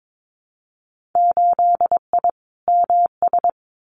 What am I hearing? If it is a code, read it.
Morse code, heard as a keyed tone, about 22 wpm, 700 Hz. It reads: 8I MS
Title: Winter Field Day Exchanges Encore 22wpm